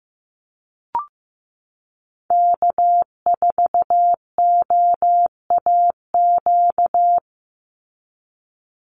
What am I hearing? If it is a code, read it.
Morse code, heard as a keyed tone, about 15 wpm, 700 Hz. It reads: K4OAQ
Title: Call Signs Repeated 3x - 15wpm